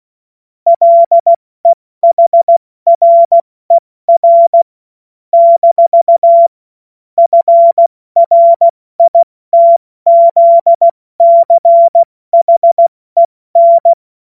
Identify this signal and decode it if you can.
Morse code, heard as a keyed tone, about 16 wpm, 680 Hz. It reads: LEHRER - FRITZCHEN
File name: Morsekurs / Witze / Witz 10. August 2021, 70 F80